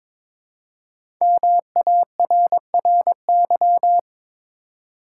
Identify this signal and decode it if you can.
Morse code, heard as a keyed tone, about 22 wpm, 700 Hz. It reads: MARRY
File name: 5 Letter Words - First 3 Letters Given - 22wpm